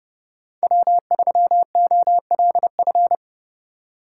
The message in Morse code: W3OLF

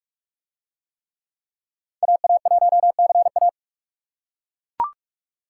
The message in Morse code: AA1CA

